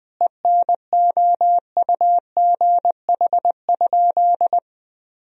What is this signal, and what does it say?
Morse code, heard as a keyed tone, about 21 wpm, 700 Hz. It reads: ENOUGH?
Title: Sentences from Top 1000 Words - Review Part 1 - 20wpm